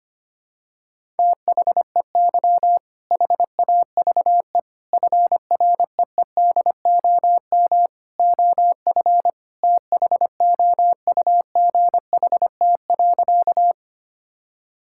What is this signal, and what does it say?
Morse code, heard as a keyed tone, about 25 wpm, 700 Hz. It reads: THEY HAVE FREEDOM OF THOUGHT.
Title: Sentences from Top 1000 Words - Review Part 5 - 25wpm